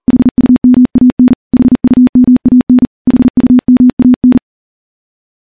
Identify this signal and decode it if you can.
Morse code, heard as a keyed tone, about 40 wpm, 250 Hz. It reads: HUMAN HUMAN HUMAN
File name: morse-training-40WPM